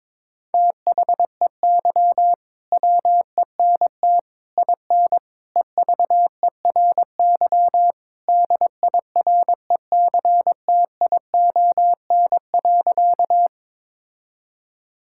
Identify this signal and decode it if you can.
Morse code, heard as a keyed tone, about 22 wpm, 700 Hz. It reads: THEY WENT IN EVERY DIRECTION.
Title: Sentences from Top 1000 Words - Review Part 5 - 22wpm